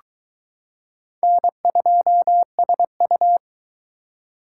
Morse code, heard as a keyed tone, about 22 wpm, 700 Hz. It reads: N2SU